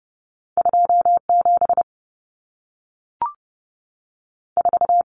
Morse code, heard as a keyed tone, about 30 wpm, 700 Hz. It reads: 2 7 4
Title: Common Cut Numbers - 2-Digit 30wpm